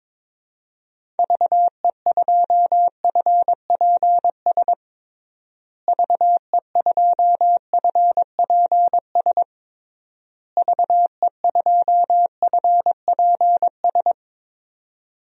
Morse code, heard as a keyed tone, about 22 wpm, 700 Hz. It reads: VE2FPS VE2FPS VE2FPS
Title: DXCC-100 Call Signs - 6 Characters - Repeated 3x - 22wpm